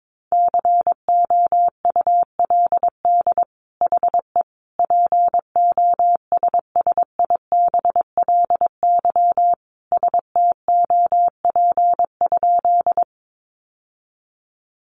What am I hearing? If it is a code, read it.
Morse code, heard as a keyed tone, about 22 wpm, 700 Hz. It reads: COULD HE POSSIBLY STOP?
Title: Sentences from Top 1300 Words 22wpm